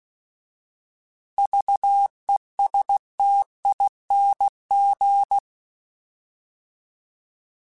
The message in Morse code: VESTING

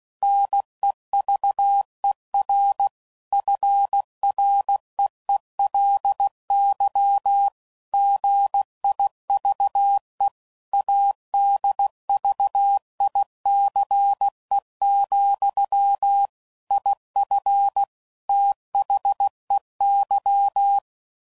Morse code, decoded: NEVER FREELY GIVE ADVICE, IF THEY

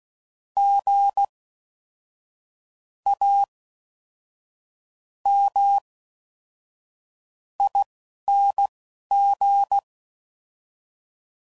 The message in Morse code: GAMING

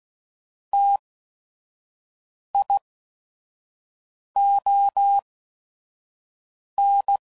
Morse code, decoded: T I O N

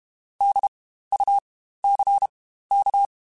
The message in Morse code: DUCK

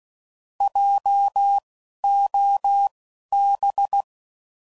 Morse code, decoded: JOB

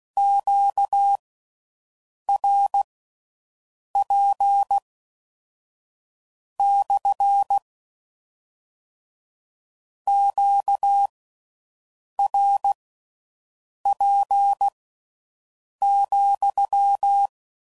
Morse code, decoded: QRP / QRP,

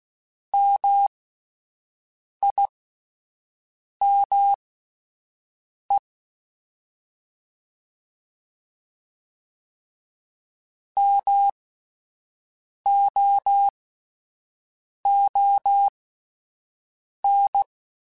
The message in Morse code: MIME MOON

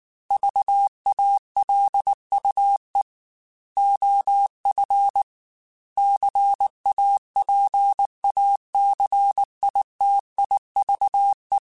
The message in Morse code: VALUE OF CAPACITIVE

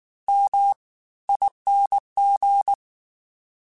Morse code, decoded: MING

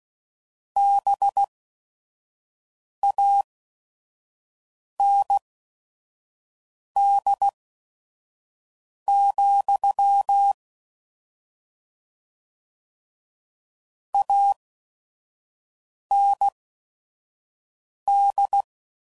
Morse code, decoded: BAND, AND

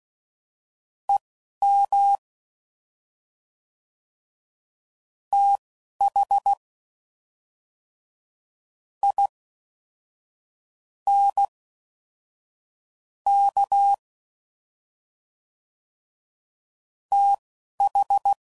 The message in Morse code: EM THINK TH

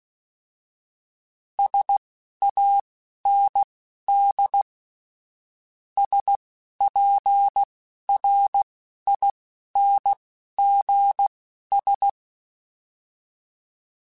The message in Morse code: SAND SPRINGS

